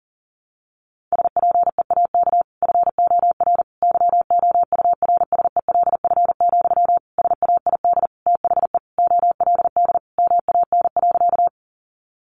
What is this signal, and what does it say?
Morse code, heard as a keyed tone, about 40 wpm, 700 Hz. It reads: SPEAK FOR YOURSELF, SAID THE OLD MAN.
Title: Sentences from Top 1100 Words 40wpm